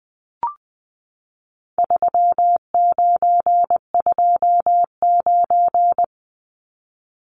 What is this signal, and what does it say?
Morse code, heard as a keyed tone, about 20 wpm, 700 Hz. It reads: 3929
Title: Numbers - 4-Digit 20wpm